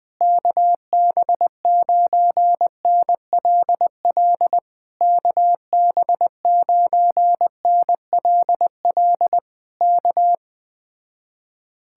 Morse code, heard as a keyed tone, about 20 wpm, 700 Hz. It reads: KB9NLL KB9NLL K
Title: Speed-Racing Common QSO Phrases Encore - 17wpm